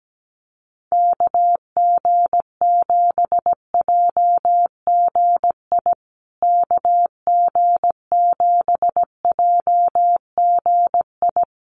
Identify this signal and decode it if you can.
Morse code, heard as a keyed tone, about 17 wpm, 700 Hz. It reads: KG7JGI KG7JGI
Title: USA Call Signs Repeated 3x 17wpm